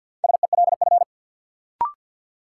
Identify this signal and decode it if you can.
Morse code, heard as a keyed tone, about 52 wpm, 700 Hz. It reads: SELF
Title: Sentences from Top 1000 Words - Review Part 1 - New Word Drill - 50wpm